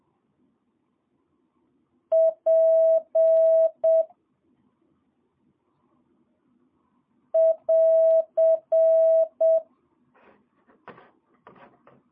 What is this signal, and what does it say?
Morse code, heard as a keyed tone, about 7 wpm, 650 Hz. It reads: P <AR>
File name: jh8cbh / 2022cw